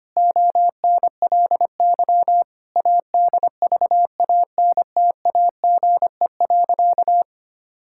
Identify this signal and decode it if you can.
Morse code, heard as a keyed tone, about 25 wpm, 700 Hz. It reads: ONLY ADVANTAGE.
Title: Sentences from Top 1000 Words - Review Part 6 - 25wpm